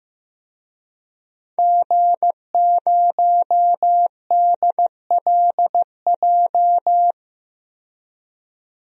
Morse code, heard as a keyed tone, about 15 wpm, 700 Hz. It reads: G0DLJ
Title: DXCC-100 Call Signs - 5 Characters - 15wpm